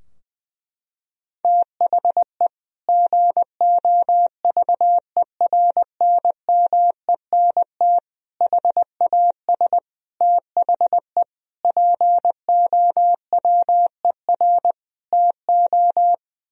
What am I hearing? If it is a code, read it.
Morse code, heard as a keyed tone, about 20 wpm, 700 Hz. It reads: THE GOVERNMENT HAS THE POWER TO
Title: Sentences from Top 1000 Words - Review Part 3 - 20wpm